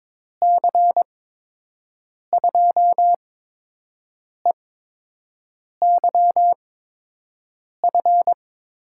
Morse code, heard as a keyed tone, about 22 wpm, spaced as for 6 wpm, 700 Hz. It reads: C2EYF
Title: USA Call Signs - 6 Characters - 8x Character Spacing - 22wpm